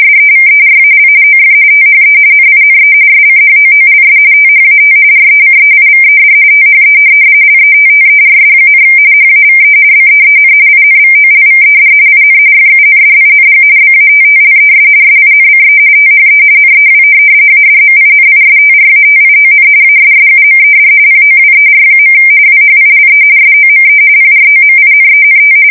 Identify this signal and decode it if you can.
RTTY, about 45.45 baud. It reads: SEVERAL OF THE NINE SCHOOLS THAT RECEIVED THE LETTERS, HAVE BEEN INVOLVED IN FUNDING BATTLES SINCE THE NEW ADMINISTRATION ASSUMED POWER. WHILE SOME PROMINENT
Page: 170 Shift 2125/2295